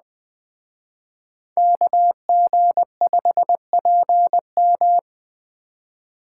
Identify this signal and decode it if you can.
Morse code, heard as a keyed tone, about 21 wpm, 700 Hz. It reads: KG5PM